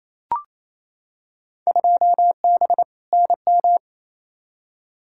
Morse code, heard as a keyed tone, about 28 wpm, 700 Hz. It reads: 2B NM
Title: Field Day Exchanges Encore - 28wpm